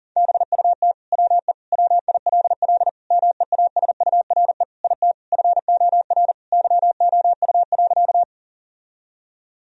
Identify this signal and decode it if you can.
Morse code, heard as a keyed tone, about 40 wpm, 700 Hz. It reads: BUT WE WILL MEASURE IT FOR YOU.
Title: Sentences from Top 1000 Words - Review Part 2 - 40wpm